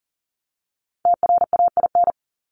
Morse code, heard as a keyed tone, about 40 wpm, 700 Hz. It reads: TRAIN
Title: ING Suffix - Top 100 Words with Base Words 40wpm